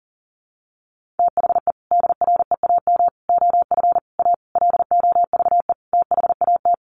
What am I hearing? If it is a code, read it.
Morse code, heard as a keyed tone, about 40 wpm, 700 Hz. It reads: THE DREAM OF A LOVE THAT
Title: Sentences from Top 1500 Words 40wpm